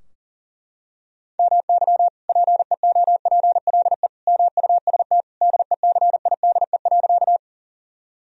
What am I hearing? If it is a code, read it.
Morse code, heard as a keyed tone, about 40 wpm, 700 Hz. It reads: MY PEOPLE MUST DECIDE.
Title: Sentences from Top 1000 Words - Review Part 3 - 40wpm